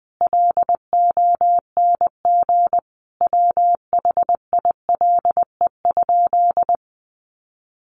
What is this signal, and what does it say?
Morse code, heard as a keyed tone, about 20 wpm, 700 Hz. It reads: LONG WHILE?
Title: Sentences from Top 300 Words 20wpm